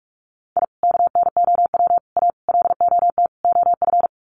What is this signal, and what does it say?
Morse code, heard as a keyed tone, about 44 wpm, 700 Hz. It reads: I KNOW A LOT OF
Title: Sets of 5 Words Encore 45wpm